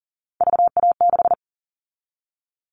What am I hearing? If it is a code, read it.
Morse code, heard as a keyed tone, about 42 wpm, 700 Hz. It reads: VA6
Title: Callsign Prefixes for Top 100 DXCC - 40wpm